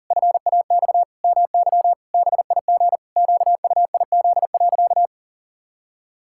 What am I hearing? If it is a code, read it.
Morse code, heard as a keyed tone, about 40 wpm, 700 Hz. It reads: FAX MY BIG QUIZ.
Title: Pangram Challenge v2 - 40wpm